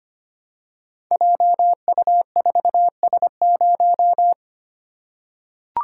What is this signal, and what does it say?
Morse code, heard as a keyed tone, about 25 wpm, 700 Hz. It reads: JU4S0
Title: Code Groups - 5 Letter:Numbers - 25wpm